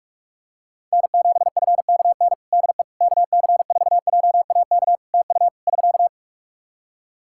Morse code, heard as a keyed tone, about 46 wpm, 700 Hz. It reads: N7FKN DE KC4JAK TU <SK>
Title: Practice QSO - Exchange 4 - Ending - 45wpm